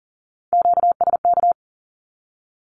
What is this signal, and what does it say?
Morse code, heard as a keyed tone, about 40 wpm, 700 Hz. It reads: QSK
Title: Q-Codes 40wpm